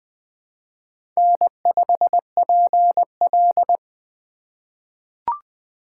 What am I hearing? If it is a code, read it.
Morse code, heard as a keyed tone, about 20 wpm, 700 Hz. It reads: N5PL
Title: USA Call Signs - 4 Characters - 4x Character Spacing - 20wpm